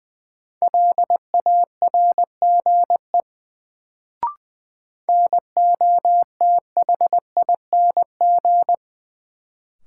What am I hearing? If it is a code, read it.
Morse code, heard as a keyed tone, about 21 wpm, 700 Hz. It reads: LARGE NOTHING
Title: Sets of 4 Words - New Word Drill - 20wpm